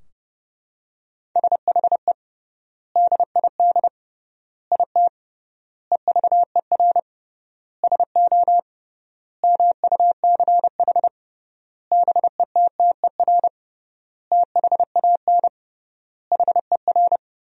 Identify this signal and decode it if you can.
Morse code, heard as a keyed tone, about 28 wpm, 700 Hz. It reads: SHE DID IT EVER SO MUCH BETTER THAN HER
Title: 3x Word-Spacing - Sentences from Top 400 Words - 30wpm